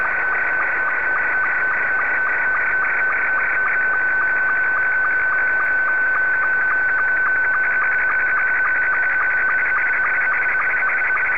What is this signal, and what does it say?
SSTV (slow-scan television), Scottie 2.